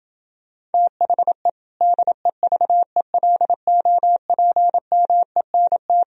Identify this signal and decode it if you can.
Morse code, heard as a keyed tone, about 28 wpm, 700 Hz. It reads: THE DEVELOPMENT